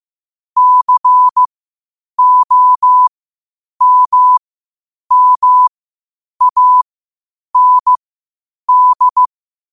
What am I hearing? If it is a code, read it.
Morse code, heard as a keyed tone, about 15 wpm, spaced as for 8 wpm, 1000 Hz. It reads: COMMAND
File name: qotdfiles / 8wpm